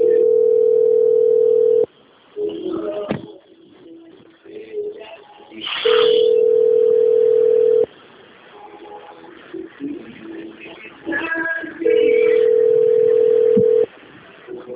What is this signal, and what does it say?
Ringback tone (North America), 440 + 480 Hz.
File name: out- - - - - .6770